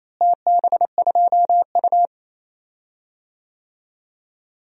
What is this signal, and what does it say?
Morse code, heard as a keyed tone, about 28 wpm, 700 Hz. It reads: TB2U
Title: DXCC-100 Call Signs - 4 Characters - Repeated 2x - 28wpm